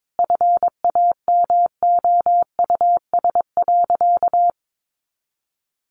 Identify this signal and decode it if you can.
Morse code, heard as a keyed tone, about 22 wpm, 700 Hz. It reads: FAMOUS.
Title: Sentences from Top 1500 Words 22wpm